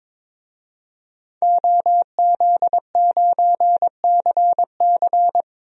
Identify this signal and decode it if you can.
Morse code, heard as a keyed tone, about 22 wpm, 700 Hz. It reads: OZ9CC
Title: DXCC-100 Call Signs - 5 Characters - Repeated 2x - 22wpm